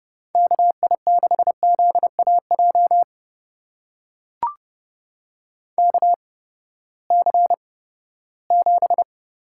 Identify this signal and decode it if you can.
Morse code, heard as a keyed tone, about 31 wpm, spaced as for 17 wpm, 700 Hz. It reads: KI6ZAJ KC7
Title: USA Call Signs - 6 Characters - 8x Character Spacing - 30wpm